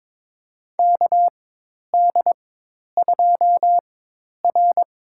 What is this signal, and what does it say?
Morse code, heard as a keyed tone, about 22 wpm, spaced as for 10 wpm, 700 Hz. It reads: KD2R
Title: USA Call Signs - 4x Character Spacing - 22wpm